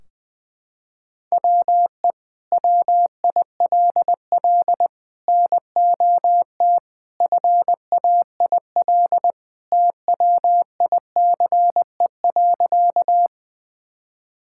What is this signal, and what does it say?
Morse code, heard as a keyed tone, about 20 wpm, 700 Hz. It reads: WE WILL NOT FAIL TWICE.
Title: Sentences from Top 1600 Words 20wpm